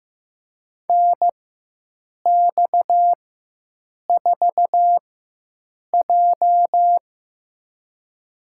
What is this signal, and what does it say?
Morse code, heard as a keyed tone, about 16 wpm, spaced as for 7 wpm, 700 Hz. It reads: NX4J
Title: USA Call Signs - 4x Character Spacing - 15wpm